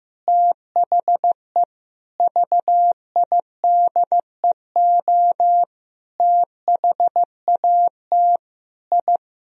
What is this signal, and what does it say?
Morse code, heard as a keyed tone, about 15 wpm, 700 Hz. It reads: THE VIDEO THAT I
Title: Sentences from Top 1000 Words - Review Part 4 - 15wpm